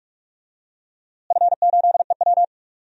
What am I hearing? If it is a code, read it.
Morse code, heard as a keyed tone, about 41 wpm, 700 Hz. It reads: F8EW
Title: DXCC-100 Call Signs - 4 Characters - Repeated 3x - 45wpm